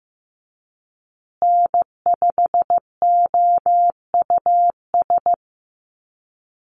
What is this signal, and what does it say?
Morse code, heard as a keyed tone, about 15 wpm, 700 Hz. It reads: N5OUS
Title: USA Call Signs Repeated 2x 15wpm